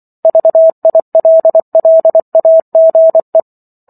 Morse code, heard as a keyed tone, about 24 wpm, 650 Hz. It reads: VILLAGE